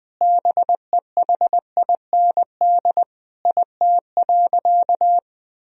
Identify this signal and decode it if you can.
Morse code, heard as a keyed tone, about 20 wpm, 700 Hz. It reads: BEHIND IT.
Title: Sentences from Top 500 Words and Abbreviations 20wpm